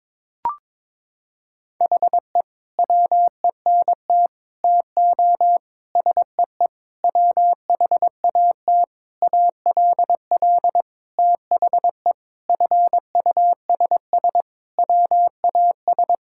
Morse code, heard as a keyed tone, about 22 wpm, 700 Hz. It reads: HE WENT TO SEE WHAT ALL THE FUSS WAS
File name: practice / 22wpm / Sentences with 8-Word Phrases - 22wpm